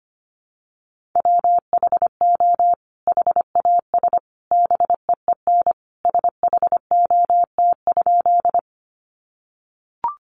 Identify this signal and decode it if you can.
Morse code, heard as a keyed tone, about 25 wpm, 700 Hz. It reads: WHO HAS BEEN SHOT?
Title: Sentences from Top 1100 Words 25wpm